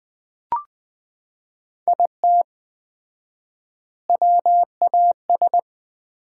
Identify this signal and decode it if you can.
Morse code, heard as a keyed tone, about 20 wpm, 700 Hz. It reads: IT WAS
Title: 4x Word-Spacing - Sets of 3 Words - 20wpm